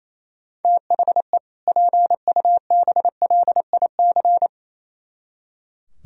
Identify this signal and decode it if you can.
Morse code, heard as a keyed tone, about 28 wpm, 700 Hz. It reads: THE PUBLIC
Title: Speed-Racing - Sets of 2 Words 28wpm